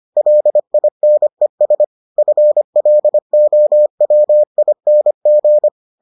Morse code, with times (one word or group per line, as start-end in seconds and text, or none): 0.2-1.8 LINES
2.2-5.7 FLOWING